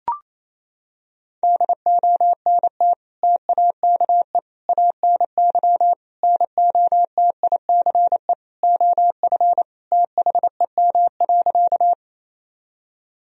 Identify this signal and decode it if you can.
Morse code, heard as a keyed tone, about 27 wpm, 700 Hz. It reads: DONT TAKE ANY NOTICE OF THEM.
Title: Sentences from Top 1000 Words - Review Part 2 - 28wpm